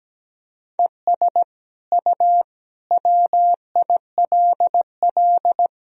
Morse code, heard as a keyed tone, about 17 wpm, 700 Hz. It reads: ES U WILL
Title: Sentences from Top 400 Words and Abbreviations 17wpm